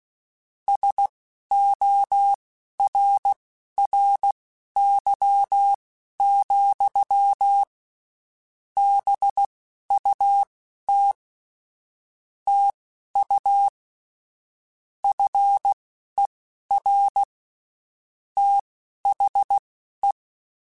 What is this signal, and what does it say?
Morse code, heard as a keyed tone, about 16 wpm, spaced as for 12 wpm, 800 Hz. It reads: SORRY, BUT TU FER THE